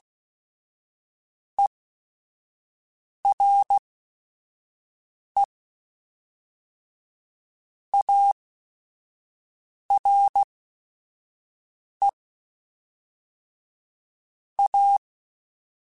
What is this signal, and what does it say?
Morse code, heard as a keyed tone, about 16 wpm, spaced as for 5 wpm, 800 Hz. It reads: ERE ARE A
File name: Technical Subjects / Technical Subjects-mp3s / Sw Pwr Sup @5